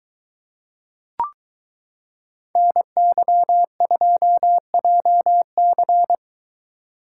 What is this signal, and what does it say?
Morse code, heard as a keyed tone, about 23 wpm, 700 Hz. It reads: NY2JC